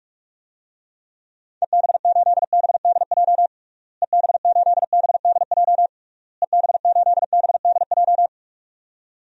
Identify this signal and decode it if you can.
Morse code, heard as a keyed tone, about 46 wpm, 700 Hz. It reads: EB8BDJ EB8BDJ EB8BDJ
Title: DXCC-100 Call Signs - 6 Characters - Repeated 3x - 45wpm